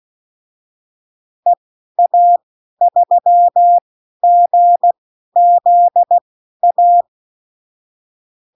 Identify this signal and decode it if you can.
Morse code, heard as a keyed tone, about 16 wpm, spaced as for 12 wpm, 700 Hz. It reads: EA3GZA
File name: CWconcurso QSO libroCW4 16-8ppm